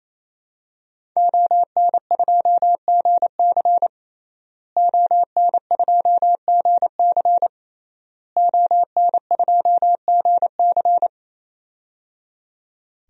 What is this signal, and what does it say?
Morse code, heard as a keyed tone, about 28 wpm, 700 Hz. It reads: ON2GC ON2GC ON2GC
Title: DXCC-100 Call Signs - 5 Characters - Repeated 3x - 28wpm